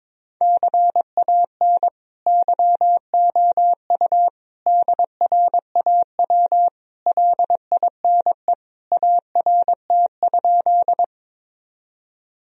Morse code, heard as a keyed tone, about 22 wpm, 700 Hz. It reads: CAN YOU DRAW LINE ART?
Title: Sentences from Top 1000 Words - Review Part 3 - 22wpm